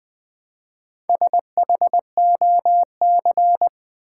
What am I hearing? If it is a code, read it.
Morse code, heard as a keyed tone, about 20 wpm, 700 Hz. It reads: SHOC
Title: Build A Word - 1 Letter At A Time - 5-Letter Words - 20wpm